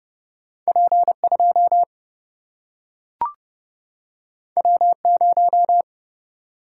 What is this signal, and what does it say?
Morse code, heard as a keyed tone, about 30 wpm, 700 Hz. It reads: P 2 W 0
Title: Single-Letter Single-Number 30wpm